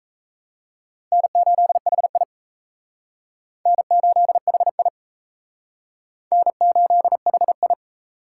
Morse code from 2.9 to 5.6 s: N8HI

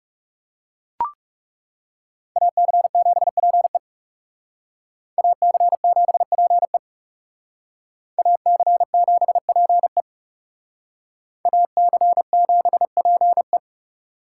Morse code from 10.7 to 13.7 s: AC7PE

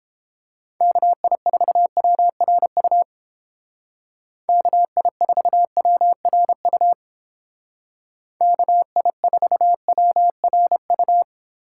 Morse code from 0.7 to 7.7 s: KI4WRU KI4WRU

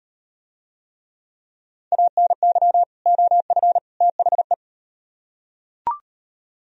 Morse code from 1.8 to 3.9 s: ANY OF

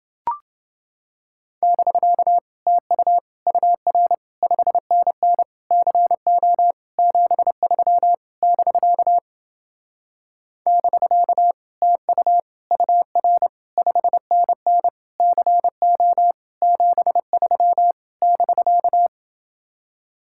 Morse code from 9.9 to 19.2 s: <BK> TU UR 5NN CO 73 <BK>

